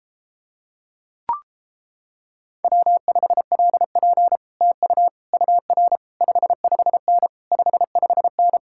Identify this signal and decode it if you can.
Morse code, heard as a keyed tone, about 29 wpm, 700 Hz. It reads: W5LP TU UR 55N 55N